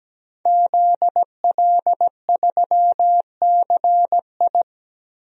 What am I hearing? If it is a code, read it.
Morse code, heard as a keyed tone, about 17 wpm, 700 Hz. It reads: ZL3CI
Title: DXCC-100 Call Signs - 5 Characters - Repeated 2x - 17wpm